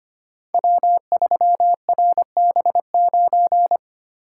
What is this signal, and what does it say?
Morse code, heard as a keyed tone, about 25 wpm, 700 Hz. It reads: W3RB9
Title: Code Groups - 5 Letter:Numbers - 25wpm